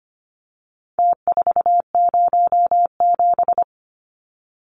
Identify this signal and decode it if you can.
Morse code, heard as a keyed tone, about 25 wpm, 700 Hz. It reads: T407
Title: Numbers - Contest Style - 4-Digit 25wpm